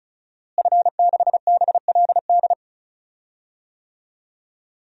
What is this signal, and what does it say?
Morse code, heard as a keyed tone, about 36 wpm, 700 Hz. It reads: F6BLD
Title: DXCC-100 Call Signs - 5 Characters - Repeated 2x - 35wpm